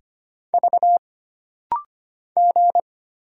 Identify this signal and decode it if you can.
Morse code, heard as a keyed tone, about 25 wpm, 700 Hz. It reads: V G
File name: Single Character (Letter:Number:Punctuation:Prosign) - Rapid-Fire - 25wpm